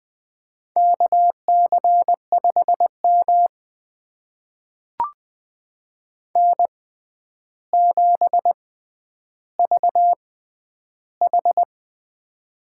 Morse code, heard as a keyed tone, about 20 wpm, spaced as for 10 wpm, 700 Hz. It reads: KC5M N7VH
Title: USA Call Signs - 4 Characters - 6x Character Spacing - 20wpm